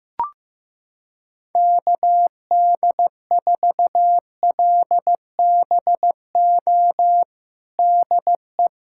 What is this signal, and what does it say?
Morse code, heard as a keyed tone, about 15 wpm, 700 Hz. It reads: KD4LBO DE
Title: USA Callsign DE Callsign - 6 Characters - 15wpm